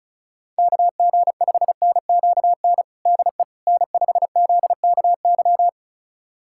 Morse code, heard as a keyed tone, about 34 wpm, 700 Hz. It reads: KG5NQN DE N5ZKY
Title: USA Callsign DE Callsign 35wpm